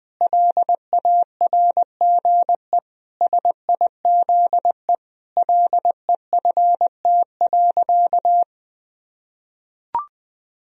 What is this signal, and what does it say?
Morse code, heard as a keyed tone, about 20 wpm, 700 Hz. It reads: LARGE SIZE LEFT.
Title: Sentences from Top 1000 Words - Review Part 2 - 20wpm